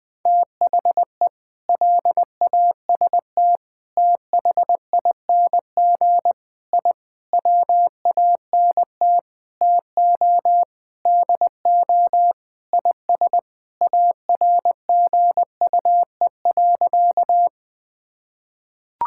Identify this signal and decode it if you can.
Morse code, heard as a keyed tone, about 20 wpm, 700 Hz. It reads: THE LAST THING I WANT TO DO IS ARGUE.
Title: Sentences with 7-Word Phrases - 20wpm